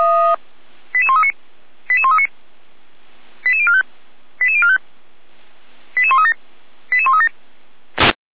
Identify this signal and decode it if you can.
Five-tone selective call (ZVEI), heard as quick runs of five tones, tones of about 70 ms. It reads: 88139 88139 88045 88045 88137 88137